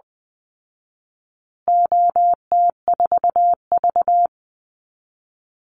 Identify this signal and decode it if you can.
Morse code, heard as a keyed tone, about 20 wpm, 700 Hz. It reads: OT4V
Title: Call Signs Repeated 3x - 20wpm